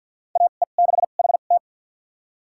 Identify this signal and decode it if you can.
Morse code, heard as a keyed tone, about 53 wpm, spaced as for 37 wpm, 690 Hz. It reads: AE6HT
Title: USA Call Signs - 2x Character Spacing - 50wpm